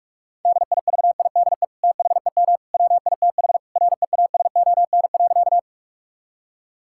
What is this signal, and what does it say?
Morse code, heard as a keyed tone, about 41 wpm, 700 Hz. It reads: DIVIDE THEM WITH REASON.
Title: Sentences from Top 1600 Words 45wpm